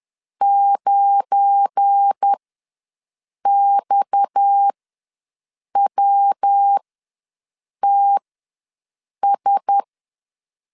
Morse code, heard as a keyed tone, about 10 wpm, spaced as for 6 wpm, 800 Hz. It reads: 9XWTS
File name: LEZIONI / 7 GF9Z / 7 GF9 ws 10wpm 4034